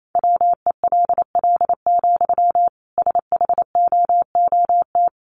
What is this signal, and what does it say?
Morse code, heard as a keyed tone, about 28 wpm, 700 Hz. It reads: WELL, SHOOT
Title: Sentences from Top 1200 Words 28wpm